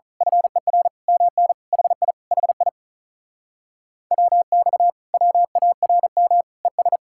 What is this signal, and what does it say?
Morse code, heard as a keyed tone, about 32 wpm, 700 Hz. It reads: FER MN HI HI WX WARM ES
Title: Speed-Racing Common QSO Phrases - 35wpm